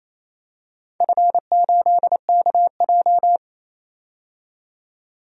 Morse code, heard as a keyed tone, about 28 wpm, 700 Hz. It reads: F8KJ
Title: DXCC-100 Call Signs - 4 Characters - 28wpm